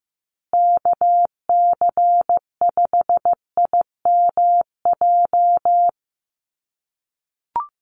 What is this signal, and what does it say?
Morse code, heard as a keyed tone, about 15 wpm, 700 Hz. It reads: KC5IMJ
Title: USA Call Signs Repeated 3x 15wpm